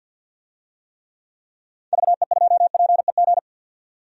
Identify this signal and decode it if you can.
Morse code, heard as a keyed tone, about 51 wpm, 700 Hz. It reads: VE2PEG
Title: DXCC-100 Call Signs - 6 Characters - 50wpm